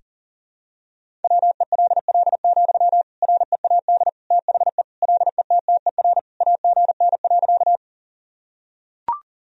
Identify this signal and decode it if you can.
Morse code, heard as a keyed tone, about 40 wpm, 700 Hz. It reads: WELL, READ THE LETTER AGN.
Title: Sentences from Top 300 Words and Abbreviations 40wpm